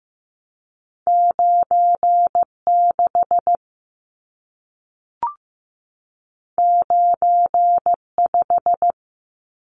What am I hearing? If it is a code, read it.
Morse code, heard as a keyed tone, about 15 wpm, 700 Hz. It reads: 96 95
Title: Numbers - 2-Digit 15wpm